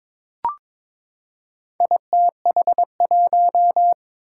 Morse code, heard as a keyed tone, about 23 wpm, 700 Hz. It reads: ITH1
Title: Four Challenging Letter-Number Combinations 22wpm